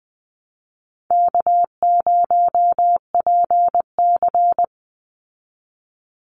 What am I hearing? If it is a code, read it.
Morse code, heard as a keyed tone, about 20 wpm, 700 Hz. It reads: K0PC
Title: Easy Call Signs Repeated 2x - 20wpm